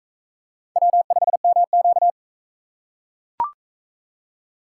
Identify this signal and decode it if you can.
Morse code, heard as a keyed tone, about 38 wpm, 700 Hz. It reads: W5MQ